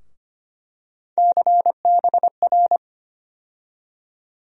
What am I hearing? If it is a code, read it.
Morse code, heard as a keyed tone, about 25 wpm, 700 Hz. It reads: CBR
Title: USA Call Sign Postfixes - 3 Letters 25wpm